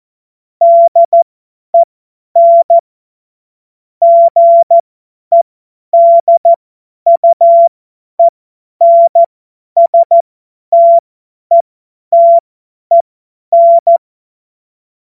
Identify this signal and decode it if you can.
Morse code, heard as a keyed tone, about 14 wpm, spaced as for 10 wpm, 680 Hz. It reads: DEN GEDUENSTETEN
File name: Morsekurs / Witze / Witz 23. August 2022 50 F70